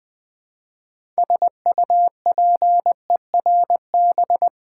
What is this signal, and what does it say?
Morse code, heard as a keyed tone, about 20 wpm, 700 Hz. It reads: SUPERB
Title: Build A Word - 3 Letters At A Time - 6-Letter Words - 20wpm